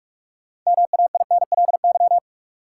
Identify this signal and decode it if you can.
Morse code, heard as a keyed tone, about 43 wpm, 700 Hz. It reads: MAINLY
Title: Sentences from Top 1400 Words - New Word Drill - 45wpm